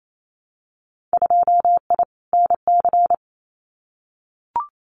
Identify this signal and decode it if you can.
Morse code, heard as a keyed tone, about 28 wpm, 700 Hz. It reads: 2I NC
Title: Winter Field Day Exchanges 28wpm